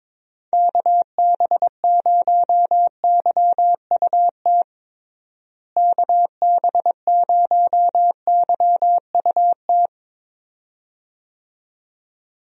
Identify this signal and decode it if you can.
Morse code, heard as a keyed tone, about 22 wpm, 700 Hz. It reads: KB0YUT KB0YUT
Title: USA Call Signs - 6 Characters - Repeated 2x - 22wpm